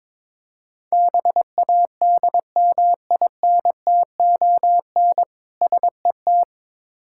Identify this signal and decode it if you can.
Morse code, heard as a keyed tone, about 22 wpm, 690 Hz. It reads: BADMINTON SET
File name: Difficult Words with Hints - 22wpm